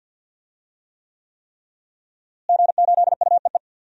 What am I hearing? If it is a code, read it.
Morse code, heard as a keyed tone, about 47 wpm, 700 Hz. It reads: G8UEE
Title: DXCC-100 Call Signs - 5 Characters - Repeated 3x - 50wpm